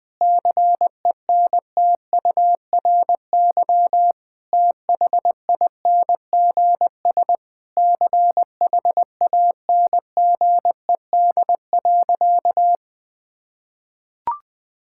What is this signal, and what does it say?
Morse code, heard as a keyed tone, about 20 wpm, 700 Hz. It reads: CENTURY THINGS CHANGED.
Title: Sentences with 4-Word Phrases - 20wpm